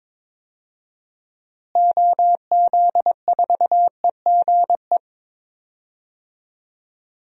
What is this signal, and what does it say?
Morse code, heard as a keyed tone, about 22 wpm, 700 Hz. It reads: OZ4EGE
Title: DXCC-100 Call Signs - 6 Characters - Repeated 3x - 22wpm